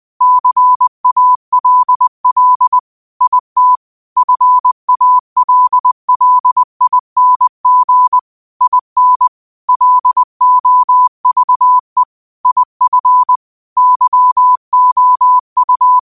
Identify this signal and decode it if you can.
Morse code, heard as a keyed tone, about 20 wpm, 1000 Hz. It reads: CALL IT FALLING IN LOVE IF YOU
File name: qotdfiles / 20wpm